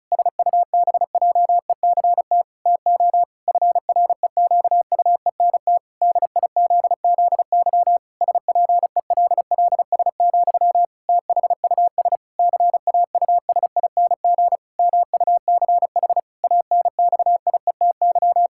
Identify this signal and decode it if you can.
Morse code, heard as a keyed tone, about 34 wpm, 700 Hz. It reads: SUBJECT TO FREQUENT DIZZY SPELLS, THUS CAUSING MUCH ANXIETY
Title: Pangram Challenge v2 - 35wpm